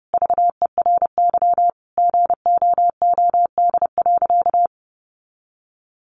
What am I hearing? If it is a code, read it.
Morse code, heard as a keyed tone, about 30 wpm, 700 Hz. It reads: VERY GOOD.
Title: Sentences from Top 1400 Words 30wpm